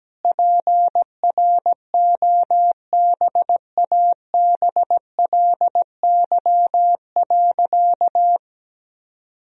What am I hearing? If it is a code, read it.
Morse code, heard as a keyed tone, about 17 wpm, 700 Hz. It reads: PROBABLY.
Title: Sentences from Top 1000 Words - Review Part 4 - 17wpm